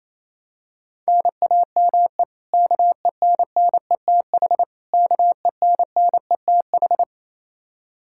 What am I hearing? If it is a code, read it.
Morse code, heard as a keyed tone, about 28 wpm, 700 Hz. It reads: NAME KENNETH KENNETH
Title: Practice QSO - Exchange 1 - Beginning - 28wpm